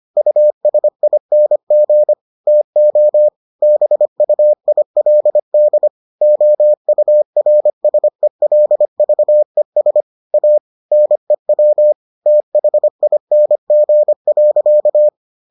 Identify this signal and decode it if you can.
Morse code, heard as a keyed tone, about 25 wpm, 600 Hz. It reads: USING TO BUILD OURSELVES A NEW THING.